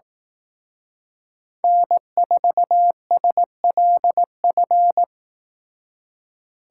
Morse code, heard as a keyed tone, about 18 wpm, 700 Hz. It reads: N4SLF
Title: Speed Racing - USA 1x3 Call Signs 15wpm